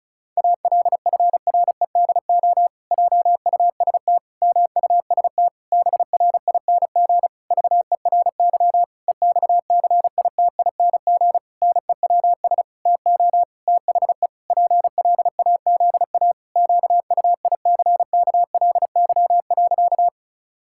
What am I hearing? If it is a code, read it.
Morse code, heard as a keyed tone, about 36 wpm, 700 Hz. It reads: ALFREDO JUST MUST BRING VERY EXCITING NEWS TO THE PLAZA QUICKLY.
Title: Pangram Challenge v2 - 35wpm